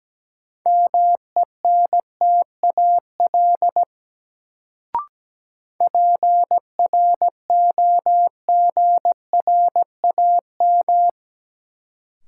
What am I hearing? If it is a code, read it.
Morse code, heard as a keyed tone, about 17 wpm, 700 Hz. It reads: MENTAL PROGRAM
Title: Sentences from Top 1000 Words - Review Part 4 - New Word Drill - 17wpm